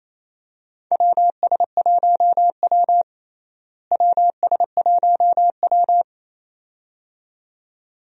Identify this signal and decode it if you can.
Morse code, heard as a keyed tone, about 28 wpm, 700 Hz. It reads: WS1W WS1W
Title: USA Call Signs - 4 Characters - Repeated 2x - 28wpm